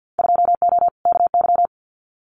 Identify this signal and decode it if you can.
Morse code, heard as a keyed tone, about 50 wpm, 700 Hz. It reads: 2O KY
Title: Winter Field Day Exchanges 50wpm